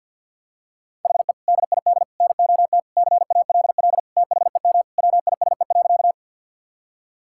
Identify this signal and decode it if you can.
Morse code, heard as a keyed tone, about 52 wpm, 700 Hz. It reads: HE DID NOT CALL THEM WISE.
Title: Sentences from Top 1600 Words 50wpm